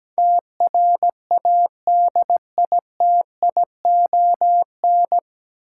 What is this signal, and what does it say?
Morse code, heard as a keyed tone, about 17 wpm, 700 Hz. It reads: TRADITION
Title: Sentences from Top 1500 Words - New Word Drill - 17wpm